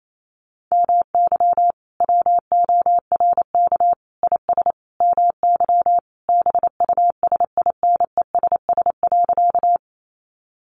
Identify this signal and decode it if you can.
Morse code, heard as a keyed tone, about 28 wpm, 700 Hz. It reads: MY WORK IS MY BUSINESS.
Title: Sentences from Top 600 Words 28wpm